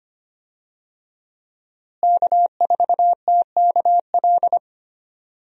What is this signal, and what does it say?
Morse code, heard as a keyed tone, about 25 wpm, 700 Hz. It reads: K4TKL
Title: USA Call Signs - 5 Characters - Repeated 2x - 25wpm